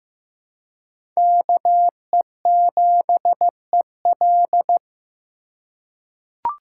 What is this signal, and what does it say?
Morse code, heard as a keyed tone, about 15 wpm, 700 Hz. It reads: KE7EL
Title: USA Call Signs - 5 Characters - 2x Character Spacing - 15wpm